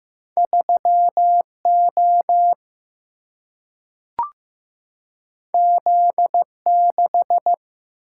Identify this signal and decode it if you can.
Morse code, heard as a keyed tone, about 15 wpm, 700 Hz. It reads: 3O Z6
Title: Single-Letter Single-Number 15wpm